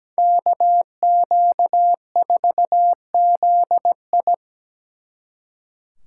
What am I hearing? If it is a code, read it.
Morse code, heard as a keyed tone, about 20 wpm, 700 Hz. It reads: KQ4ZI
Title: Speed Racing - USA 2x2 Call Signs 17wpm